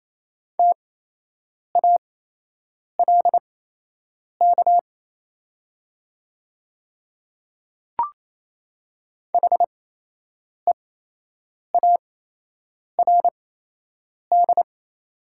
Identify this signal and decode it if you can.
Morse code, heard as a keyed tone, about 27 wpm, spaced as for 8 wpm, 690 Hz. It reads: TALK HEARD
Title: 8x Character Spacing - Top 500 Words - 28wpm